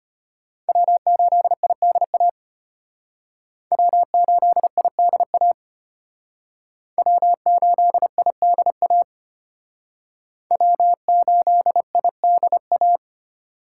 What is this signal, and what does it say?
Morse code, heard as a keyed tone, about 32 wpm, 700 Hz. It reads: W8IDA W8IDA W8IDA W8IDA